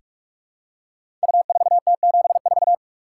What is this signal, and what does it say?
Morse code, heard as a keyed tone, about 43 wpm, 700 Hz. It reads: U4T74
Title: Code Groups - 5 Letter:Numbers - 45wpm